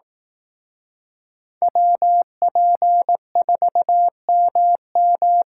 Morse code, heard as a keyed tone, about 18 wpm, 700 Hz. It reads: WP4MM
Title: Speed Racing - USA 2x2 Call Signs 15wpm